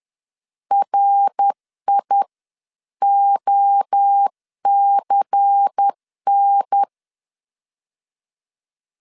Morse code, heard as a keyed tone, about 10 wpm, spaced as for 9 wpm, 800 Hz. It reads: RIOCN
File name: CW / DET / 1 6 1500C S C 10wpm 4252